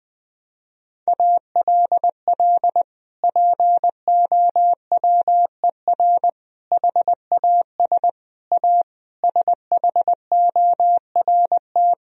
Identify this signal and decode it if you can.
Morse code, heard as a keyed tone, about 20 wpm, 700 Hz. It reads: ALL POWER HAS A SHORT
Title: Sentences from Top 1000 Words - Review Part 2 - 20wpm